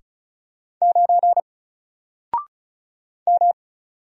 Morse code, heard as a keyed tone, about 36 wpm, 700 Hz. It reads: 9 M
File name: Single-Letter-Number-35wpm